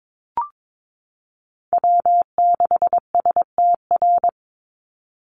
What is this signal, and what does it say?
Morse code, heard as a keyed tone, about 22 wpm, 700 Hz. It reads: W6STR
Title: Call Signs - 22wpm